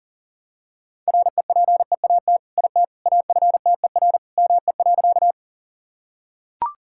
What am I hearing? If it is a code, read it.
Morse code, heard as a keyed tone, about 40 wpm, 700 Hz. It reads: REPEAT IT AFTER ME.
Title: Sentences from Top 1600 Words 40wpm